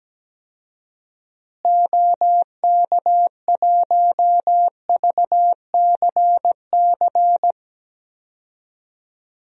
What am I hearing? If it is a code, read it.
Morse code, heard as a keyed tone, about 17 wpm, 700 Hz. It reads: OK1VCC
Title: DXCC-100 Call Signs - 6 Characters - Repeated 2x - 17wpm